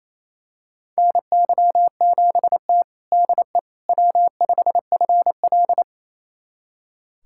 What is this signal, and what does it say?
Morse code, heard as a keyed tone, about 28 wpm, 700 Hz. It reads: NY7T DE W5FL
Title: USA Callsign DE Callsign - 4 Characters - 28wpm